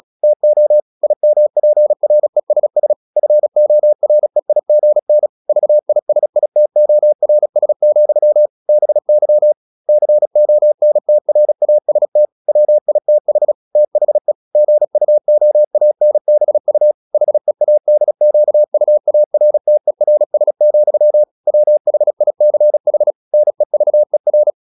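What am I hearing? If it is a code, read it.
Morse code, heard as a keyed tone, about 36 wpm, 600 Hz. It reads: TO IMPRESS FOREIGN VISITORS, BY CONTRAST WITH THE GUOANBU HEADQUARTERS, WHICH NEVER
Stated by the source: Morse practice at 36 wpm, text of Ken Follett's novel Never, from CW med Gnister.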